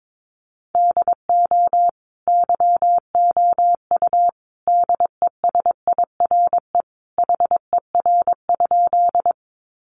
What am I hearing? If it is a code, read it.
Morse code, heard as a keyed tone, about 22 wpm, 700 Hz. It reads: DO YOU DESIRE HER?
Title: Sentences from Top 1100 Words 22wpm